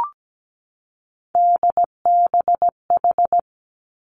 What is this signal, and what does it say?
Morse code, heard as a keyed tone, about 17 wpm, 700 Hz. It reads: DBH
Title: Three Challenging Letter-Number Combinations 17wpm